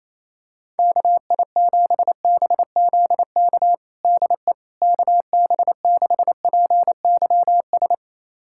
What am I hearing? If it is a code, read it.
Morse code, heard as a keyed tone, about 28 wpm, 700 Hz. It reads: KI7BZK DE KB6PYS
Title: USA Callsign DE Callsign - 6 Characters - 28wpm